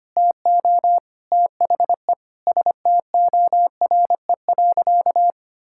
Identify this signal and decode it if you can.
Morse code, heard as a keyed tone, about 25 wpm, 700 Hz. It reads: TO THE STORE.
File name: practice / 25wpm / Sentences with 4-Word Phrases - 25wpm